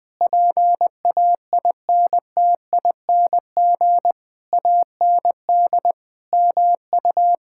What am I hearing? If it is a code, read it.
Morse code, heard as a keyed tone, about 21 wpm, 700 Hz. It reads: PAINTING AND MU
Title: Sentences from Top 1000 Words - Review Part 6 - 20wpm